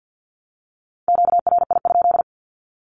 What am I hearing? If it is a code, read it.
Morse code, heard as a keyed tone, about 50 wpm, 700 Hz. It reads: QRI?
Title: Q-Codes 50wpm